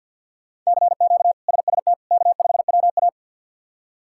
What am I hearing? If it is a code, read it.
Morse code, heard as a keyed tone, about 50 wpm, 700 Hz. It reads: CQ SST K5WA
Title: K1USN SST QSO Practice - 50wpm